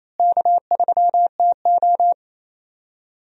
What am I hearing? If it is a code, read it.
Morse code, heard as a keyed tone, about 28 wpm, 700 Hz. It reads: K3TO
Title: USA Call Signs - 4 Characters - Repeated 3x - 28wpm